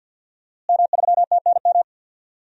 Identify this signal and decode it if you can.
Morse code, heard as a keyed tone, about 49 wpm, 700 Hz. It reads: M3TNM
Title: DXCC-100 Call Signs - 5 Characters - Repeated 2x - 50wpm